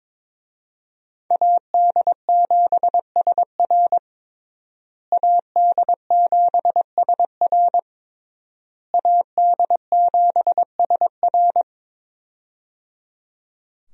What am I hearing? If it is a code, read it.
Morse code, heard as a keyed tone, about 22 wpm, 700 Hz. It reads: AD7SR AD7SR AD7SR
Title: USA Call Signs - 5 Characters - Repeated 3x - 22wpm